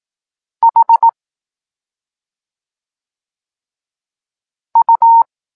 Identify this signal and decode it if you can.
Morse code, heard as a keyed tone, about 18 wpm, 920 Hz. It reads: H U
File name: Tekst-9-rownomierny